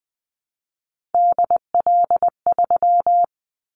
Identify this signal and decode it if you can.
Morse code, heard as a keyed tone, about 20 wpm, 700 Hz. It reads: DL3
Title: Callsign Prefixes for Top 25 DXCC - Repeated 3x - 20wpm